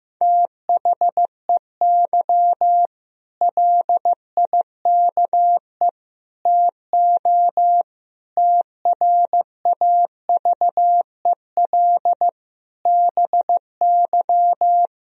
Morse code, heard as a keyed tone, about 16 wpm, 700 Hz. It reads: THEY LIKE TO TRAVEL BY
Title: Sentences from Top 1000 Words - Review Part 3 - 15wpm